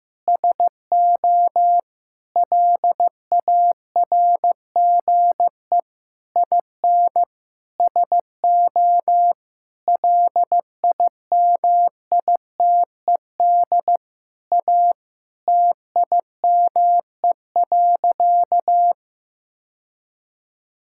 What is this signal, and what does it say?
Morse code, heard as a keyed tone, about 15 wpm, 700 Hz. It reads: SO LARGE IN SO LIMITED A TIME.
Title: Sentences from Top 1000 Words - Review Part 6 - 15wpm